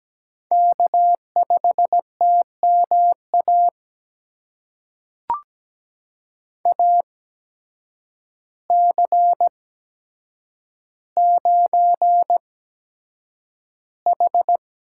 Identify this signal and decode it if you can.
Morse code, heard as a keyed tone, about 17 wpm, spaced as for 8 wpm, 700 Hz. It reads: K5TMA AC9H
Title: USA Call Signs - 5 Characters - 8x Character Spacing - 17wpm